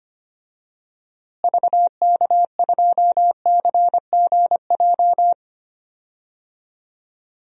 Morse code, heard as a keyed tone, about 25 wpm, 700 Hz. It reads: VK2CGJ
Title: DXCC-100 Call Signs - 6 Characters - Repeated 3x - 25wpm